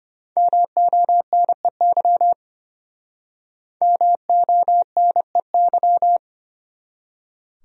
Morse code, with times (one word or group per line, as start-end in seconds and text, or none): 0.4-2.3 MONEY
3.8-6.2 MONEY